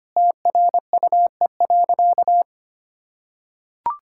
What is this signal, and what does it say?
Morse code, heard as a keyed tone, about 25 wpm, 700 Hz. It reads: TRUE.
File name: practice / 25wpm / Sentences with 4-Word Phrases - 25wpm